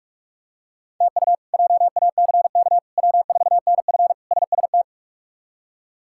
Morse code, heard as a keyed tone, about 43 wpm, 700 Hz. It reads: TU JACK W4NF SST
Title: K1USN SST QSO Practice - 45wpm